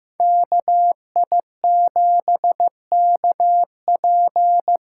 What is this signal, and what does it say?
Morse code, heard as a keyed tone, about 15 wpm, 700 Hz. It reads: KI7KP
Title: USA Call Signs - 6 Characters - 6x Character Spacing - 15wpm